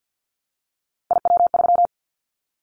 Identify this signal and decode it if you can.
Morse code, heard as a keyed tone, about 50 wpm, 700 Hz. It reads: IW3
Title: Callsign Prefixes for Top 50 DXCC - 50wpm